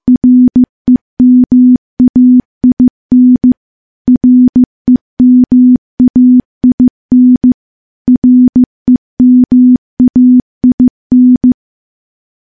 Morse code, heard as a keyed tone, about 16 wpm, 250 Hz. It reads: REMAIN REMAIN REMAIN
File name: morse / morse-training-15WPM